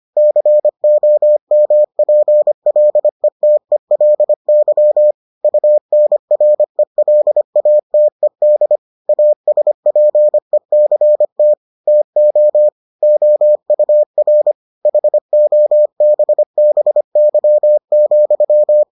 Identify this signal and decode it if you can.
Morse code, heard as a keyed tone, about 25 wpm, 600 Hz. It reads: COMPLETELY UNRELATED ASPECT TO OUR HOBBY,